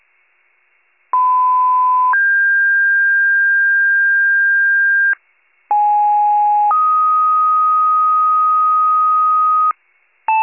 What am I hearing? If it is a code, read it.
Two-tone page, heard as a one-second tone, then a three-second tone, about 1010 then 1600 Hz, about 830 then 1250 Hz, with noise. It is heard from 1.1 to 9.7 s.